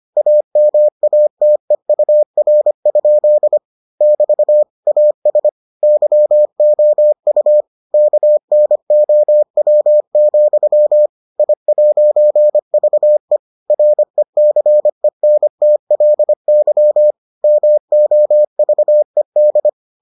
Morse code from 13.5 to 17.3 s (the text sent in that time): RECENTLY